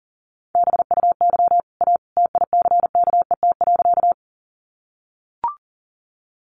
Morse code, heard as a keyed tone, about 40 wpm, 700 Hz. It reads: BUY A TICKET.
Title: Sentences from Top 1400 Words 40wpm